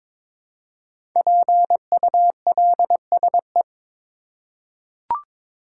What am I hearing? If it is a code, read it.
Morse code, heard as a keyed tone, about 22 wpm, 700 Hz. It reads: PULSE
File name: 5 Letter Words - First 3 Letters Given - 22wpm